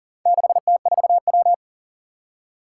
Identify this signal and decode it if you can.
Morse code, heard as a keyed tone, about 40 wpm, 700 Hz. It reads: 6T4W
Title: Four Challenging Letter-Number Combinations 40wpm